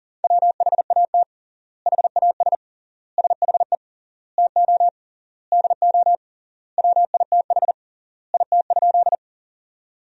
Morse code, heard as a keyed tone, about 40 wpm, 700 Hz. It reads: WHAT HAS SHE TO DO WITH IT?
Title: 3x Word-Spacing - Sentences from Top 100 Words - 40wpm